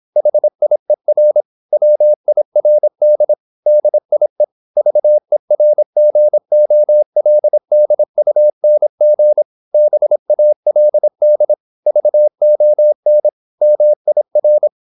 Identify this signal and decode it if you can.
Morse code, heard as a keyed tone, about 26 wpm, 600 Hz. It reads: HIER WIRD DIE VERGOLDUNG BALD VON MIR